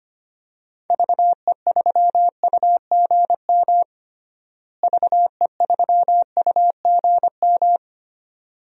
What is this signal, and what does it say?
Morse code, heard as a keyed tone, about 25 wpm, 700 Hz. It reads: VE3UGM VE3UGM
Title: DXCC-100 Call Signs - 6 Characters - Repeated 2x - 25wpm